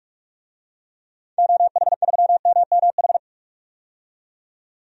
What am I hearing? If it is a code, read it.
Morse code, heard as a keyed tone, about 46 wpm, 700 Hz. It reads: OH3MMH
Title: DXCC-100 Call Signs - 6 Characters - 45wpm